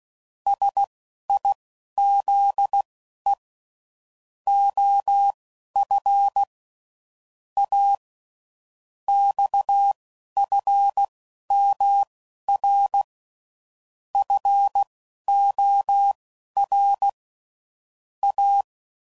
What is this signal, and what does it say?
Morse code, heard as a keyed tone, about 16 wpm, spaced as for 11 wpm, 800 Hz. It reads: SIZE OF A XFMR FOR A
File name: CW Code Groups @10WPM / Technical Subjects / SwPwrSup